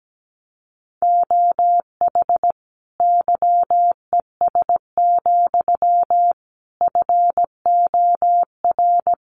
Morse code, heard as a keyed tone, about 17 wpm, 700 Hz. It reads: OH YES, FOR
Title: Sentences from Top 1300 Words 17wpm